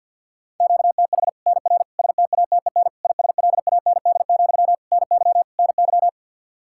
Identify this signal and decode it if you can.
Morse code, heard as a keyed tone, about 50 wpm, 700 Hz. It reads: QTH NR STATEN ISLAND, NY NY
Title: Practice QSO - Exchange 1 - Beginning - 50wpm